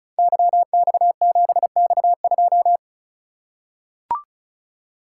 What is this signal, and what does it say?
Morse code, heard as a keyed tone, about 35 wpm, 700 Hz. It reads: YX7X2
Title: Code Groups - 5 Letter:Numbers - 35wpm